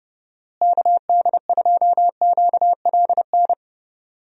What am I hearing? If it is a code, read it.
Morse code, heard as a keyed tone, about 30 wpm, 700 Hz. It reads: KD2QLN
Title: USA Call Signs - 6 Characters - Repeated 2x - 30wpm